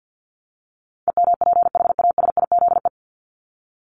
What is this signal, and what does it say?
Morse code, heard as a keyed tone, about 50 wpm, 700 Hz. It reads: EMPHASIZE
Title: Top 1251-1500 Words 50wpm